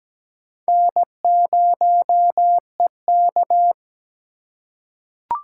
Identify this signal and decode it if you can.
Morse code, heard as a keyed tone, about 17 wpm, 700 Hz. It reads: N0EK
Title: USA Call Signs - 4x Character Spacing - 17wpm